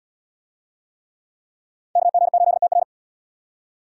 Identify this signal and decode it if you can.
Morse code, heard as a keyed tone, about 71 wpm, 700 Hz. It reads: KG8TR